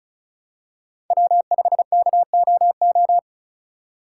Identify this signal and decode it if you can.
Morse code, heard as a keyed tone, about 36 wpm, 700 Hz. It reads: W5KOO
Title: USA Call Signs - 5 Characters - 3x Character Spacing - 35wpm